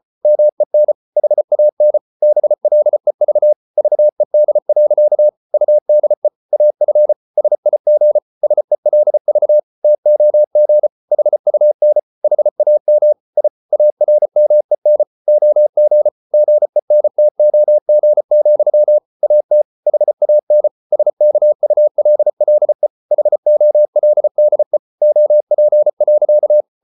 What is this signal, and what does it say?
Morse code, heard as a keyed tone, about 34 wpm, 600 Hz. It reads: MEN HAN BLEV VED. UDE AF SIG SELV TOG HUN HAM I ARMEN OG GENTOG, AT HAN SKULLE HOLDE OP.
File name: Ferie QRQ 34wpm